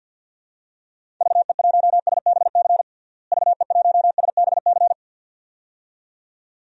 Morse code, heard as a keyed tone, about 50 wpm, 700 Hz. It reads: VE1SBC VE1SBC
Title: DXCC-100 Call Signs - 6 Characters - Repeated 2x - 50wpm